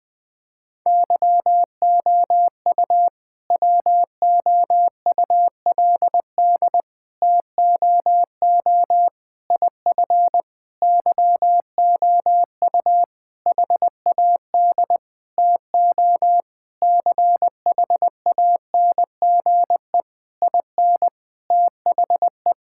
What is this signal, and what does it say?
Morse code, heard as a keyed tone, about 21 wpm, 700 Hz. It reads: YOU WOULD TOO IF YOU HAD TO CHANGE IN THE
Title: All-Age Appropriate Jokes - 20wpm